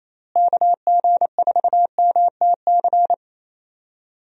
Morse code, heard as a keyed tone, about 28 wpm, 700 Hz. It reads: KG4MTC
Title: USA Call Signs - 6 Characters - Repeated 3x - 28wpm